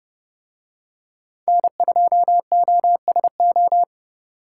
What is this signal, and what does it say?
Morse code, heard as a keyed tone, about 29 wpm, 700 Hz. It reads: N2OSO